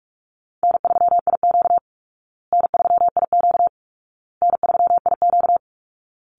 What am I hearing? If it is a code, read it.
Morse code, heard as a keyed tone, about 44 wpm, 700 Hz. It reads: N3IQ N3IQ N3IQ
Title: Easy Call Signs Repeated 3x - 45wpm